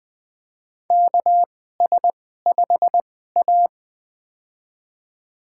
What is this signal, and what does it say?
Morse code, heard as a keyed tone, about 20 wpm, spaced as for 14 wpm, 700 Hz. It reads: KS5A
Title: USA Call Signs - 4 Characters - 2x Character Spacing - 20wpm